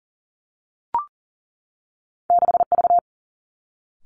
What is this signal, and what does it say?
Morse code, heard as a keyed tone, about 40 wpm, 700 Hz. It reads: 6 V
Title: Two Challenging Letter-Number Combinations 40wpm